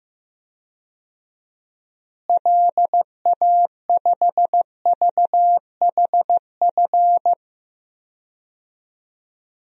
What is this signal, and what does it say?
Morse code, heard as a keyed tone, about 15 wpm, 700 Hz. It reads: LA5VHF
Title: DXCC-100 Call Signs - 6 Characters - Repeated 3x - 15wpm